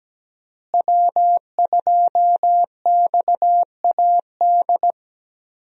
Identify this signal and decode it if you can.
Morse code, heard as a keyed tone, about 17 wpm, 700 Hz. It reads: W2XAD
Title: USA Call Signs - 5 Characters - Repeated 3x - 17wpm